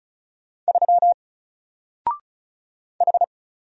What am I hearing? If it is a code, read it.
Morse code, heard as a keyed tone, about 35 wpm, 700 Hz. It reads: 3 H
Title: Single Character (Letter:Number:Punctuation:Prosign) Alt 35wpm